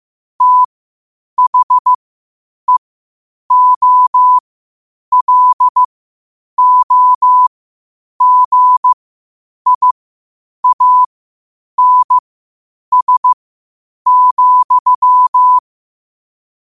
Morse code, heard as a keyed tone, about 15 wpm, spaced as for 8 wpm, 1000 Hz. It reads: THEOLOGIANS,
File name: qotdfiles / 8wpm